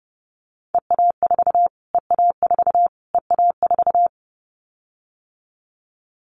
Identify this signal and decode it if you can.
Morse code, heard as a keyed tone, about 28 wpm, 700 Hz. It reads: EA4 EA4 EA4
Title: Callsign Prefixes for Top 25 DXCC - Repeated 3x - 30wpm